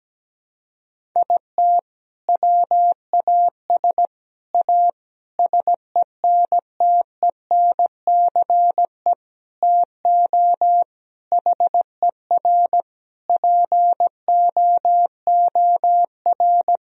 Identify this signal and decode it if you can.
Morse code, heard as a keyed tone, about 17 wpm, 700 Hz. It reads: IT WAS A SENTENCE TO HER POOR
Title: Sentences from Top 1000 Words - Review Part 1 - 17wpm